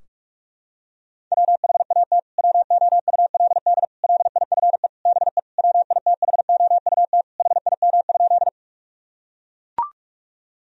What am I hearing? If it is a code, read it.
Morse code, heard as a keyed tone, about 41 wpm, 700 Hz. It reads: WHAT WOULD LIFE BE WITHOUT HIM?
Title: Sentences from Top 1000 Words - Review Part 3 - 45wpm